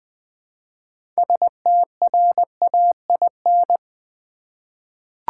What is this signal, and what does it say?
Morse code, heard as a keyed tone, about 20 wpm, 700 Hz. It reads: STRAIN
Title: Six Letter Words - 20wpm